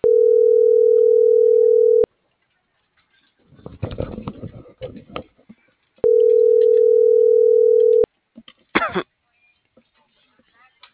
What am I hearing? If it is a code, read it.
Ringback tone (North America), 440 + 480 Hz.